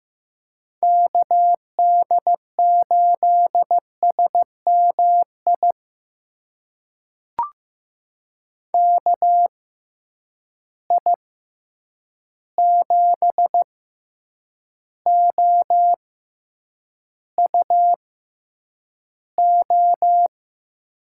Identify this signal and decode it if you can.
Morse code, heard as a keyed tone, about 15 wpm, spaced as for 8 wpm, 700 Hz. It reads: KD8SMI KI7OUO
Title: USA Call Signs - 6 Characters - 6x Character Spacing - 15wpm